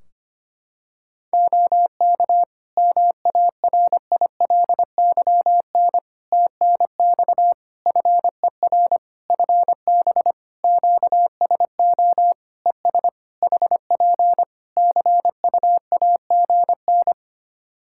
Morse code, heard as a keyed tone, about 25 wpm, 700 Hz. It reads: OK MARILYN TNX FER FB QSO ES HP CUAGN
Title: Practice QSO - Exchange 4 - Ending - 25wpm